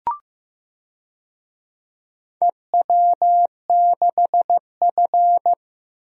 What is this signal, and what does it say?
Morse code, heard as a keyed tone, about 15 wpm, 700 Hz. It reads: EW6F
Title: DXCC-100 Call Signs - 4 Characters - Repeated 3x - 15wpm